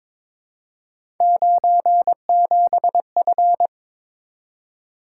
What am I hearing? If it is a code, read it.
Morse code, heard as a keyed tone, about 22 wpm, 700 Hz. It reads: 97F
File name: practice / 22wpm / Three Letter-Number Combinations - 22wpm